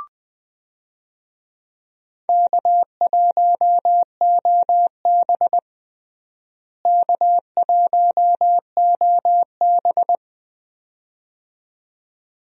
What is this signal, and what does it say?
Morse code, heard as a keyed tone, about 20 wpm, 700 Hz. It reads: K1OB K1OB
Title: USA Call Signs - 4 Characters - Repeated 2x - 20wpm